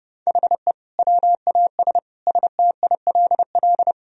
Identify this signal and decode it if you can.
Morse code, heard as a keyed tone, about 30 wpm, 700 Hz. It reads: HE WAS STILL
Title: Sentences from Top 1000 Words - Review Part 2 - 30wpm